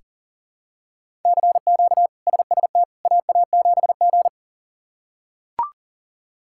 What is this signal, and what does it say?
Morse code, heard as a keyed tone, about 37 wpm, 700 Hz. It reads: CQ SST AA7G
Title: K1USN SST QSO Practice - 40wpm